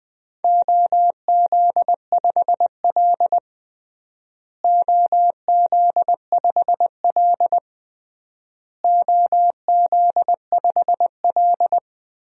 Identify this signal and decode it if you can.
Morse code, heard as a keyed tone, about 21 wpm, 700 Hz. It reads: OZ5L OZ5L OZ5L
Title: DXCC-100 Call Signs - 4 Characters - Repeated 3x - 20wpm